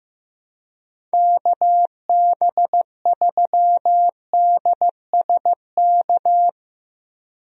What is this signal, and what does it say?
Morse code, heard as a keyed tone, about 16 wpm, 700 Hz. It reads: KB3DSK
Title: USA Call Signs - 6 Characters - Repeated 2x - 15wpm